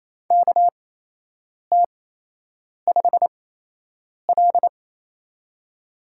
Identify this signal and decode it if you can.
Morse code, heard as a keyed tone, about 28 wpm, spaced as for 8 wpm, 700 Hz. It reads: KT5L
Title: USA Call Signs - 4 Characters - 8x Character Spacing - 28wpm